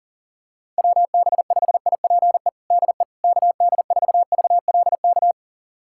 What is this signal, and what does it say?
Morse code, heard as a keyed tone, about 37 wpm, 700 Hz. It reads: WB5IPE DE KD4VLK
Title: USA Callsign DE Callsign 40wpm